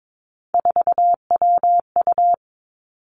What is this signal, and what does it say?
Morse code, heard as a keyed tone, about 22 wpm, 700 Hz. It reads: 4WU
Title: Three Challenging Letter-Number Combinations 22wpm